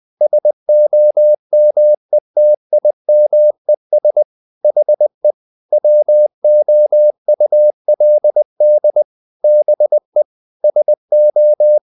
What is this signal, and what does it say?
Morse code, heard as a keyed tone, about 20 wpm, 600 Hz. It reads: SOMETIMES HE WOULD BE SO
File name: the old man and the sea / the old man and the sea part 1 20wpm